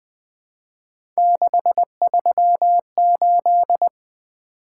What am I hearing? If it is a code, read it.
Morse code, heard as a keyed tone, about 20 wpm, 700 Hz. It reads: 638
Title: Three Letter-Number Combinations - 20wpm